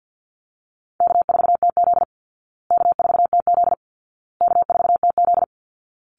Morse code, heard as a keyed tone, about 50 wpm, 700 Hz. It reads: K4TZ K4TZ K4TZ
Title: Easy Call Signs Repeated 3x - 50wpm